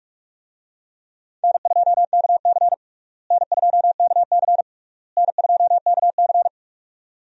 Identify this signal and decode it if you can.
Morse code, heard as a keyed tone, about 46 wpm, 700 Hz. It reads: N2KC N2KC N2KC
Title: USA Call Signs - 4 Characters - Repeated 3x - 45wpm